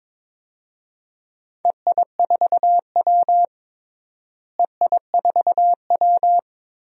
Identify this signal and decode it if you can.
Morse code, heard as a keyed tone, about 23 wpm, 700 Hz. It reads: EI4W EI4W
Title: DXCC-100 Call Signs - 4 Characters - Repeated 3x - 22wpm